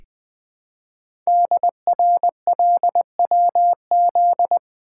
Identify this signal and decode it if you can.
Morse code, heard as a keyed tone, about 20 wpm, 700 Hz. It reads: DRLWZ
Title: Code Groups - 5 Letters - 20wpm